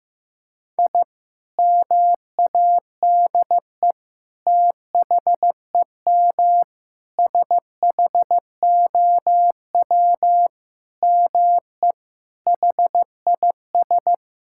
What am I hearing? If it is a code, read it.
Morse code, heard as a keyed tone, about 15 wpm, 700 Hz. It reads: I MADE THEM SHOW ME HIS
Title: Sentences from Top 1000 Words - Review Part 6 - 15wpm